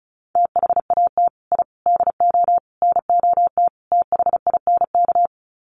Morse code, heard as a keyed tone, about 34 wpm, 700 Hz. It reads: THAT I DO NOT THINK
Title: Sets of 5 Words Encore 35wpm